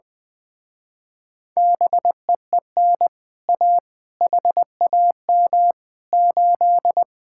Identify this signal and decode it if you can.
Morse code, heard as a keyed tone, about 20 wpm, 700 Hz. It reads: BEEN A HAM 8
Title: Speed-Racing Common QSO Phrases Encore - 20wpm